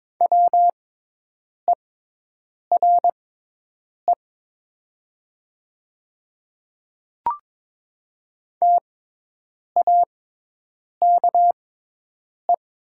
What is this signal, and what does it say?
Morse code, heard as a keyed tone, about 22 wpm, spaced as for 8 wpm, 700 Hz. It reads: WERE TAKE
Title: 6x Character Spacing - Top 200 Words - 22wpm